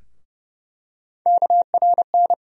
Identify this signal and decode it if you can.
Morse code, heard as a keyed tone, about 30 wpm, 700 Hz. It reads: KRN
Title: USA Call Sign Postfixes - 3 Letters 30wpm